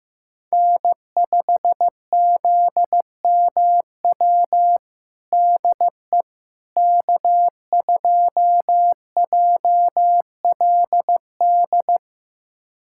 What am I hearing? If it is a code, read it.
Morse code, heard as a keyed tone, about 16 wpm, 700 Hz. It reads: N5ZMW DE K2JLD
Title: USA Callsign DE Callsign - 5 Characters - 15wpm